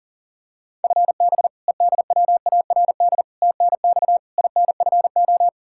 Morse code, heard as a keyed tone, about 40 wpm, 700 Hz. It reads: FB EDWARD TNX INFO
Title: Practice QSO - Exchange 3 - More Info - 40wpm